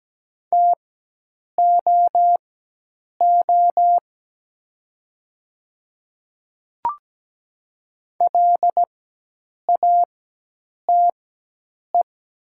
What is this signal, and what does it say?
Morse code, heard as a keyed tone, about 17 wpm, spaced as for 8 wpm, 690 Hz. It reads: TOO LATE
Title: 4x Character Spacing - Top 300 Words - 17wpm